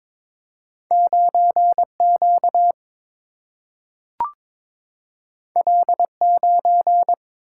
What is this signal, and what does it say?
Morse code, heard as a keyed tone, about 22 wpm, 700 Hz. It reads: 9Q L9
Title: Single-Letter Single-Number 22wpm